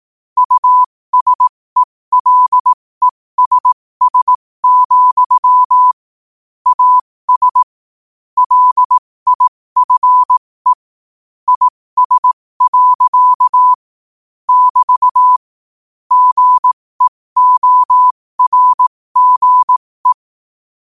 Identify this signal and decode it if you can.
Morse code, heard as a keyed tone, about 18 wpm, spaced as for 15 wpm, 1000 Hz. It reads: USELESS, AS LIFE IS. <BT> GEORGE
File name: qotdfiles / 15wpm